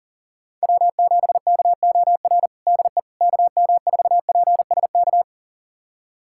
Morse code, heard as a keyed tone, about 40 wpm, 700 Hz. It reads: W7KOR DE KM4PSK
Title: USA Callsign DE Callsign 40wpm